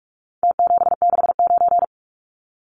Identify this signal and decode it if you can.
Morse code, heard as a keyed tone, about 44 wpm, 700 Hz. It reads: T769
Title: Numbers - Contest Style - 4-Digit 45wpm